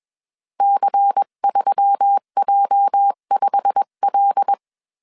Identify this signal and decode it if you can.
Morse code, heard as a keyed tone, about 21 wpm, 800 Hz. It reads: C3J5L